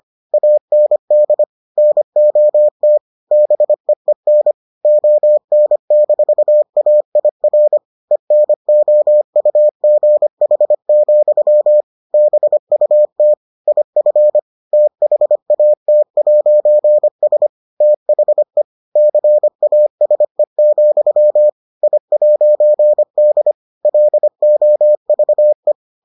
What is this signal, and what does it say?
Morse code, heard as a keyed tone, about 25 wpm, 600 Hz. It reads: AND NOT BEEN ON-AIR ENOUGH, BUT IF THAT'S THE CASE, I'D LOVE